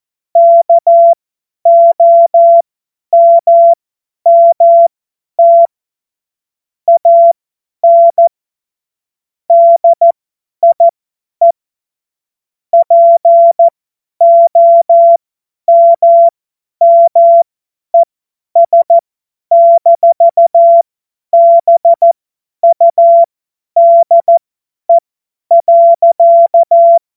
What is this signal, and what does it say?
Morse code, heard as a keyed tone, about 14 wpm, spaced as for 10 wpm, 680 Hz. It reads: KOMMT AN DIE POMMES-BUDE.
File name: Morsekurs / Witze / Witz 10.Oktober 2023 50 F70